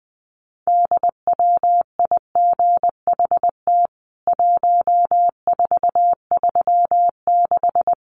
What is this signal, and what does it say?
Morse code, heard as a keyed tone, about 20 wpm, 700 Hz. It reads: DWIGHT 1436
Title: CWT Exchange 20wpm